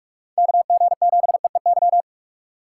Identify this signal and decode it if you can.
Morse code, heard as a keyed tone, about 46 wpm, 700 Hz. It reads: KG7EEY
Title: USA Call Signs - 2x Character Spacing - 45wpm